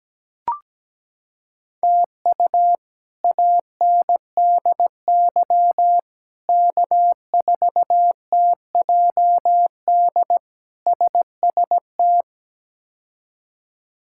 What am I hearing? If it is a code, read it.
Morse code, heard as a keyed tone, about 17 wpm, 700 Hz. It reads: TU ANDY K4TJD SST
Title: K1USN SST QSO Practice - 17wpm